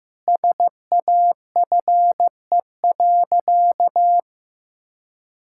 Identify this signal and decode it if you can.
Morse code, heard as a keyed tone, about 15 wpm, 700 Hz. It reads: SAFE.
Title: Sentences from Top 1600 Words 15wpm